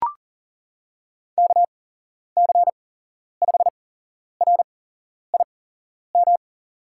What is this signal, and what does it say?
Morse code, heard as a keyed tone, about 40 wpm, spaced as for 11 wpm, 700 Hz. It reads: KC5RIM
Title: USA Call Signs - 6 Characters - 8x Character Spacing - 40wpm